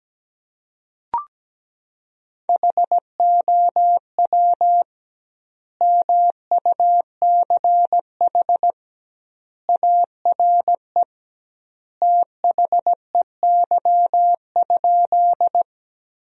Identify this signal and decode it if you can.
Morse code, heard as a keyed tone, about 17 wpm, 700 Hz. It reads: HOW MUCH ARE THEY?
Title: 2x Word-Spacing - Sentences from Top 200 Words - 17wpm